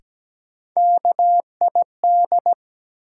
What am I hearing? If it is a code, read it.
Morse code, heard as a keyed tone, about 17 wpm, 700 Hz. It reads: KID
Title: Sets of 5 Words - New Word Drill - 17wpm